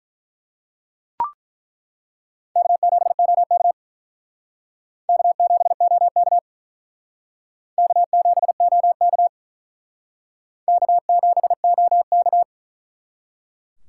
Morse code, heard as a keyed tone, about 48 wpm, 700 Hz. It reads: K7OK K7OK K7OK K7OK